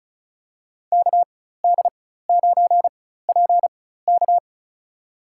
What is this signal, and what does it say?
Morse code, heard as a keyed tone, about 34 wpm, spaced as for 16 wpm, 700 Hz. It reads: KD9PK
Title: USA Call Signs - 5 Characters - 4x Character Spacing - 35wpm